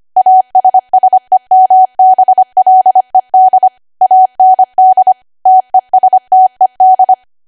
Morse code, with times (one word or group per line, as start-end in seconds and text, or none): 0.2-3.7 ASSEMBLED
4.0-5.1 AND
5.5-7.1 TESTED